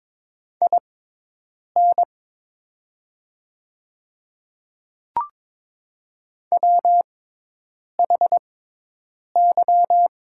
Morse code, heard as a keyed tone, about 22 wpm, spaced as for 8 wpm, 700 Hz. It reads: IN WHY
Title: 6x Character Spacing - Top 200 Words - 22wpm